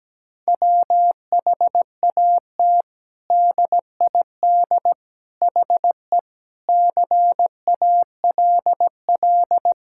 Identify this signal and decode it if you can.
Morse code, heard as a keyed tone, about 17 wpm, 700 Hz. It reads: WHAT DID HE CALL
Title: Sentences from Top 1600 Words 17wpm